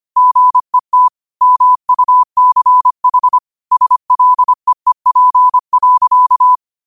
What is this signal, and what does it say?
Morse code, heard as a keyed tone, about 25 wpm, 1000 Hz. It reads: GET MUCH SLEEP.
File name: qotdfiles / 25wpm